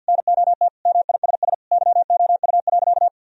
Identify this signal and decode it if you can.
Morse code, heard as a keyed tone, about 50 wpm, 700 Hz. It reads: NOT MISS YOU.
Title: Sentences from Top 1000 Words - Review Part 3 - 50wpm